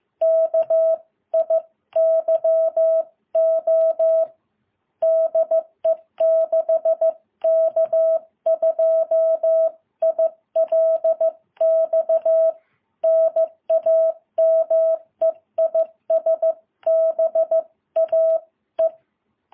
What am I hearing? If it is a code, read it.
Morse code, heard as a keyed tone, about 15 wpm, 650 Hz. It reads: KIYO DE 6K2ILX NAME IS BAE
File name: jh8cbh